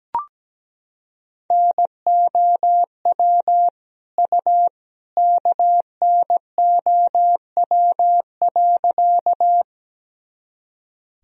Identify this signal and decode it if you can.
Morse code, heard as a keyed tone, about 17 wpm, 700 Hz. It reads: NOW U KNOW.
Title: Sentences from Top 100 Words and Abbreviations 17wpm